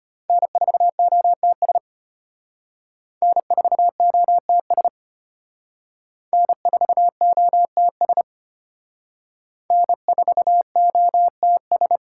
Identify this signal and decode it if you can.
Morse code, heard as a keyed tone, about 32 wpm, 700 Hz. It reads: N4OTS N4OTS N4OTS N4OTS